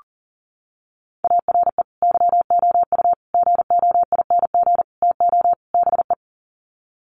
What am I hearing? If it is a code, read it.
Morse code, heard as a keyed tone, about 40 wpm, 700 Hz. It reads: ARE YOU GOING TO BE
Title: Sets of 5 Words Encore 40wpm